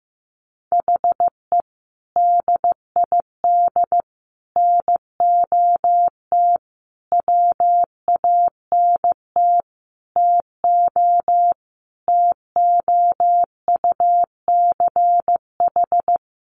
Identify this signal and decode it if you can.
Morse code, heard as a keyed tone, about 15 wpm, 700 Hz. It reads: HE DID NOT WANT TO TOUCH
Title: Sentences from Top 1200 Words 15wpm